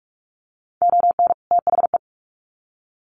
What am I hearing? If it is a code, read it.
Morse code, heard as a keyed tone, about 44 wpm, 700 Hz. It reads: ON THE
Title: Sets of 2 Words 45wpm